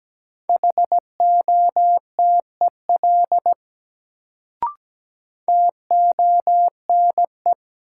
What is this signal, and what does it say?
Morse code, heard as a keyed tone, about 17 wpm, 700 Hz. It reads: HOTEL TONE
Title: Sentences from Top 1600 Words - New Word Drill Alt - 17wpm